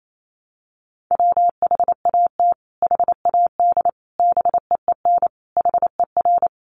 Morse code, heard as a keyed tone, about 28 wpm, 700 Hz. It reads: WHAT HAD BEEN HER
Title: Sentences from Top 1500 Words 28wpm